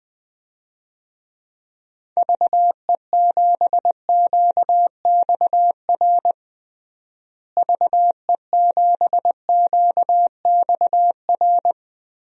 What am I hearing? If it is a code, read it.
Morse code, heard as a keyed tone, about 21 wpm, 700 Hz. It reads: VE7QXR VE7QXR
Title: DXCC-100 Call Signs - 6 Characters - Repeated 2x - 20wpm